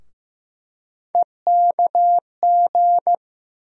Morse code, heard as a keyed tone, about 15 wpm, 700 Hz. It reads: EKG
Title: USA Call Sign Postfixes - 3 Letters 15wpm